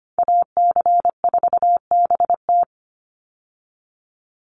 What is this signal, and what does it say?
Morse code, heard as a keyed tone, about 25 wpm, 700 Hz. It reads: AC4BT
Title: Call Signs Repeated 2x - 25wpm